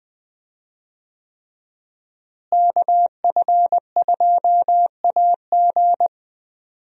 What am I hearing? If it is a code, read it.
Morse code, heard as a keyed tone, about 20 wpm, 700 Hz. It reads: KF2AG
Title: USA Call Signs - 5 Characters - Repeated 3x - 20wpm